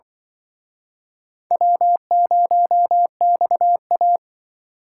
Morse code, heard as a keyed tone, about 25 wpm, 690 Hz. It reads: W0XA
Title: Speed Racing - USA 1x2 Call Signs 20wpm